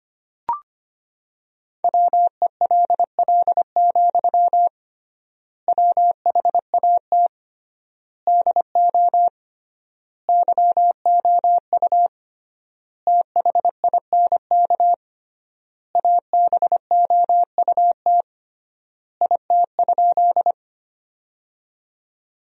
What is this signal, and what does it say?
Morse code, heard as a keyed tone, about 25 wpm, 700 Hz. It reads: WELL, WHAT DO YOU THINK ABOUT IT?
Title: 3x Word-Spacing - Sentences from Top 200 Words - 25wpm